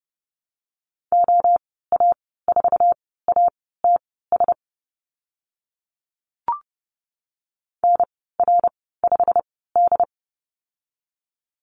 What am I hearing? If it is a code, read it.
Morse code, heard as a keyed tone, about 30 wpm, spaced as for 17 wpm, 700 Hz. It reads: OA4ATS NR5D
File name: Lesson 15g - Number 5 - Call Signs 30wpm